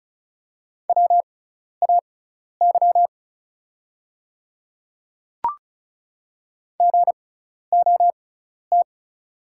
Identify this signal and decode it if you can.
Morse code, heard as a keyed tone, about 36 wpm, spaced as for 12 wpm, 700 Hz. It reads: WAY GOT
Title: 6x Character Spacing - Top 300 Words - 35wpm